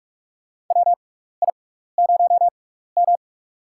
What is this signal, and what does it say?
Morse code, heard as a keyed tone, about 47 wpm, spaced as for 16 wpm, 700 Hz. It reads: WI0M
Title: USA Call Signs - 4 Characters - 6x Character Spacing - 45wpm